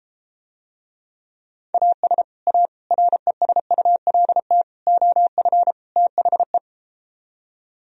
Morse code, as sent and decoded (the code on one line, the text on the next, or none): .- ... / .- / .-. ... ..- .-.. - / --- ..-. / - .... .
AS A RESULT OF THE